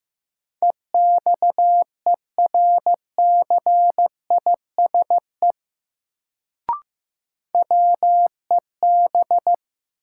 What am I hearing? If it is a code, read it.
Morse code, heard as a keyed tone, about 15 wpm, 700 Hz. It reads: EXERCISE WEB
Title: Sentences from Top 1000 Words - Review Part 4 - New Word Drill - 15wpm